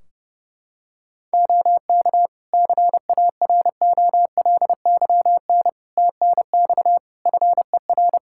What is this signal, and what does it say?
Morse code, heard as a keyed tone, about 30 wpm, 700 Hz. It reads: OK CAROLYN TNX FER
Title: Practice QSO - Exchange 4 - Ending - 30wpm